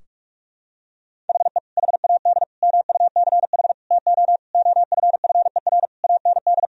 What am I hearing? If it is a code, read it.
Morse code, heard as a keyed tone, about 46 wpm, 700 Hz. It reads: HE HAD MUCH TO OFFER AND
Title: Sentences from Top 1000 Words - Review Part 4 - 45wpm